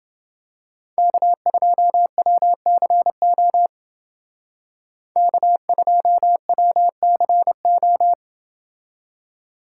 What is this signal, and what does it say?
Morse code, heard as a keyed tone, about 29 wpm, 700 Hz. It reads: K2WCO K2WCO